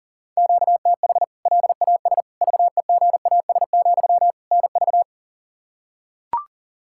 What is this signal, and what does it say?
Morse code, heard as a keyed tone, about 40 wpm, 700 Hz. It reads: QTH LAS VEGAS, NV
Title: K1USN SST QSO Practice 40wpm